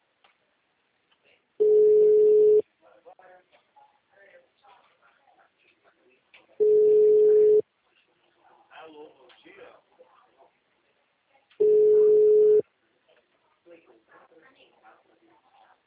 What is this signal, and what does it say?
Ringback tone (Europe, 425 Hz), with speech.